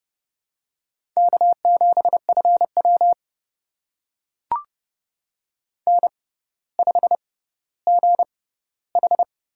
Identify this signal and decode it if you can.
Morse code, heard as a keyed tone, about 29 wpm, spaced as for 15 wpm, 700 Hz. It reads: K7FW N5GH
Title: USA Call Signs - 4 Characters - 6x Character Spacing - 30wpm